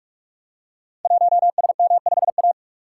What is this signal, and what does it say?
Morse code, heard as a keyed tone, about 44 wpm, 700 Hz. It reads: 1SM5A
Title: Code Groups - 5 Letter:Numbers - 45wpm